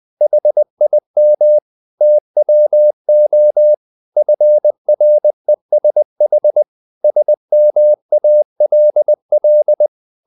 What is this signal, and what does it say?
Morse code, heard as a keyed tone, about 20 wpm, 600 Hz. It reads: HIM TWO FRESH SMALL
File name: the old man and the sea / the old man and the sea part 1 20wpm